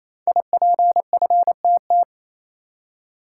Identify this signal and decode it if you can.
Morse code, heard as a keyed tone, about 29 wpm, 700 Hz. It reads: IPFTT
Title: Code Groups - 5 Letters - 28wpm